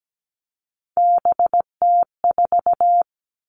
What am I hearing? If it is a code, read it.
Morse code, heard as a keyed tone, about 17 wpm, 700 Hz. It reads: BT4
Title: All Cut Numbers - 3-Digit 17wpm